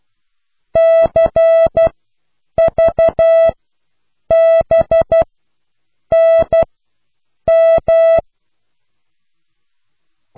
Morse code, heard as a keyed tone, about 12 wpm, spaced as for 7 wpm, 660 Hz. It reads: CVBNM